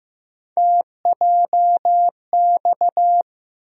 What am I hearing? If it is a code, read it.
Morse code, heard as a keyed tone, about 15 wpm, 700 Hz. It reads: TJX
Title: USA Call Sign Postfixes - 3 Letters 15wpm